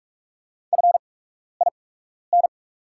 Morse code, heard as a keyed tone, about 44 wpm, spaced as for 12 wpm, 700 Hz. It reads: FIN
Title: 8x Character Spacing - Top 100 Words - 45wpm